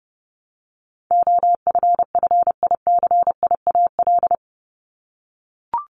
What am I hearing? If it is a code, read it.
Morse code, heard as a keyed tone, about 30 wpm, 700 Hz. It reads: OFFICIAL
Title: Top 751-1000 Words 30wpm